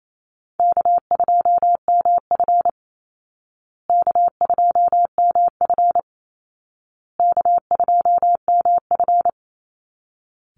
Morse code, heard as a keyed tone, about 28 wpm, 700 Hz. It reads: K2MF K2MF K2MF
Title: Easy Call Signs Repeated 3x - 28wpm